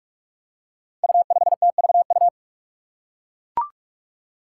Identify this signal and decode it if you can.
Morse code, heard as a keyed tone, about 42 wpm, 700 Hz. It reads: U5TVU
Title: Five Challenging Letter-Number Combinations 45wpm